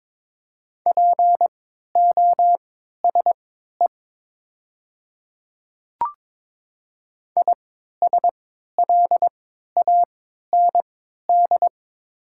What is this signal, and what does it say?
Morse code, heard as a keyed tone, about 22 wpm, spaced as for 12 wpm, 700 Hz. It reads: POSE ISLAND
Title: 3x Character Spacing - Top 500 Words - 22wpm